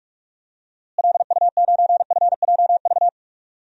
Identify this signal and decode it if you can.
Morse code, heard as a keyed tone, about 46 wpm, 700 Hz. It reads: LU9FJV
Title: DXCC-100 Call Signs - 6 Characters - Repeated 3x - 45wpm